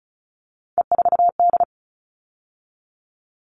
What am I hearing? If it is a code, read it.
Morse code, heard as a keyed tone, about 34 wpm, 700 Hz. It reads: E4D
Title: All Cut Numbers - 3-Digit 35wpm